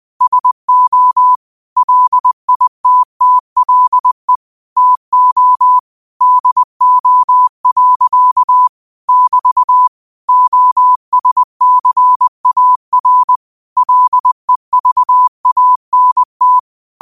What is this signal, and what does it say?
Morse code, heard as a keyed tone, about 20 wpm, 1000 Hz. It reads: SO LITTLE TO DO. <BT> OSCAR LEVANT
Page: Quotes for Tue, 12 Aug 2025 in Morse Code at 20 words per minute.